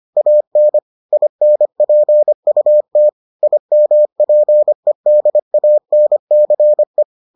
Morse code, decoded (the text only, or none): AN INPUT IMPEDANCE